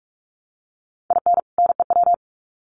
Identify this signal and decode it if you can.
Morse code, heard as a keyed tone, about 44 wpm, 700 Hz. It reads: IN NEW
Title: Sets of 2 Words 45wpm